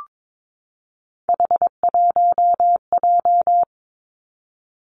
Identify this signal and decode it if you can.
Morse code, heard as a keyed tone, about 22 wpm, 700 Hz. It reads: H1J
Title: Three Challenging Letter-Number Combinations 22wpm